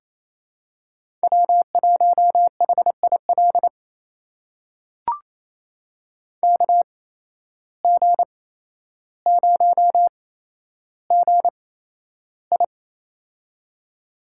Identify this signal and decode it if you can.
Morse code, heard as a keyed tone, about 28 wpm, spaced as for 12 wpm, 700 Hz. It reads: W1HIL KG0GI
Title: USA Call Signs - 5 Characters - 8x Character Spacing - 28wpm